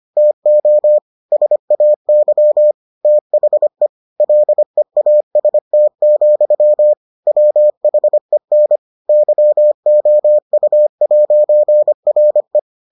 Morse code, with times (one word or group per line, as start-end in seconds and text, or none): 0.2-1.0 TO
1.3-2.7 SAY
3.0-3.9 THE
4.2-6.9 LEAST,
7.3-8.8 WHEN
9.1-12.6 YOU'RE